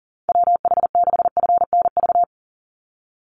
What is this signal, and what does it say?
Morse code, heard as a keyed tone, about 40 wpm, 700 Hz. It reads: WH6FNV
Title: USA Call Signs 40wpm